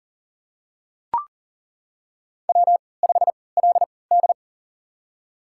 Morse code, heard as a keyed tone, about 40 wpm, spaced as for 23 wpm, 700 Hz. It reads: W5LD
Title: USA Call Signs - 4 Characters - 3x Character Spacing - 40wpm